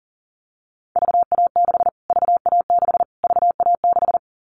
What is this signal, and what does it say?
Morse code, heard as a keyed tone, about 40 wpm, 700 Hz. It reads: VA6 VA6 VA6
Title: Callsign Prefixes for Top 25 DXCC - Repeated 3x - 40wpm